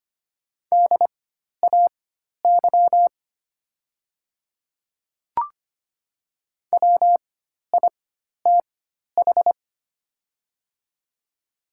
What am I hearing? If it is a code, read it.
Morse code, heard as a keyed tone, about 25 wpm, spaced as for 12 wpm, 700 Hz. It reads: DAY WITH
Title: 4x Character Spacing - Top 100 Words - 25wpm